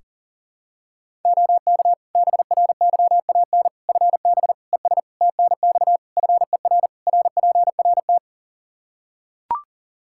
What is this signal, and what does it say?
Morse code, heard as a keyed tone, about 40 wpm, 700 Hz. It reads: OK BRYAN FB ES TNX FER RPRT
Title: Practice QSO - Exchange 2 - Info - 40wpm